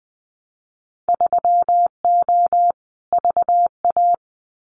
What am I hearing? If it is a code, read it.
Morse code, heard as a keyed tone, about 20 wpm, 700 Hz. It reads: 3O VA
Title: Winter Field Day Exchanges 20wpm